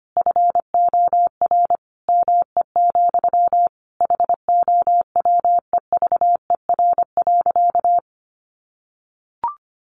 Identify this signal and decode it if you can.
Morse code, heard as a keyed tone, about 25 wpm, 700 Hz. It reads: FOR ME, HOWEVER.
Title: Sentences from Top 600 Words 25wpm